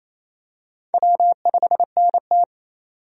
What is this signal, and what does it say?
Morse code, heard as a keyed tone, about 28 wpm, 700 Hz. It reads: W5NT
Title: USA Call Signs - 4 Characters - 28wpm